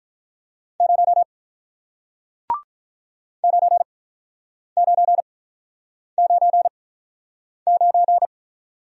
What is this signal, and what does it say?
Morse code, heard as a keyed tone, about 48 wpm, spaced as for 9 wpm, 700 Hz. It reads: 0 9999